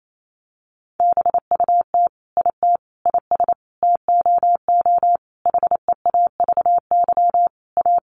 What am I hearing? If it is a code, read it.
Morse code, heard as a keyed tone, about 28 wpm, 700 Hz. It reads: BUT IT IS TOO HEAVY A
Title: Sentences from Top 1100 Words 28wpm